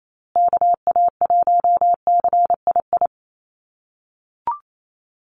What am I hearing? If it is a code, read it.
Morse code, heard as a keyed tone, about 28 wpm, 700 Hz. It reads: KA1CII
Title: USA Call Signs Repeated 3x 28wpm